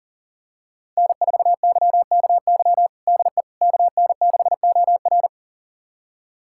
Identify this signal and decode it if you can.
Morse code, heard as a keyed tone, about 40 wpm, 700 Hz. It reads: N4YKY DE KN6OR
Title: USA Callsign DE Callsign 40wpm